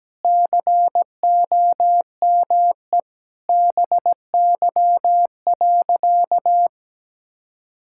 Morse code, heard as a keyed tone, about 17 wpm, 700 Hz. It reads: COME BY.
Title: Sentences from Top 1000 Words - Review Part 5 - 17wpm